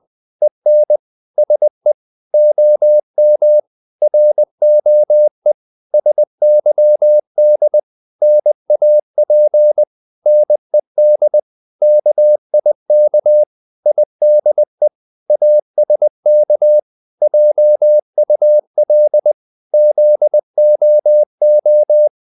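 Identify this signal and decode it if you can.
Morse code, heard as a keyed tone, about 20 wpm, 600 Hz. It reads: EN SE OM ROE SYD NAP NED KIK IDE ASK JUL ZOO
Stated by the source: Morse code practice of short Danish words at 20 wpm, from CW med Gnister.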